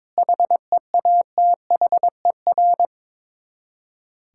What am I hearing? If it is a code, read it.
Morse code, heard as a keyed tone, about 22 wpm, 700 Hz. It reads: HEATHER
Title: K1USN SST QSO Practice 22wpm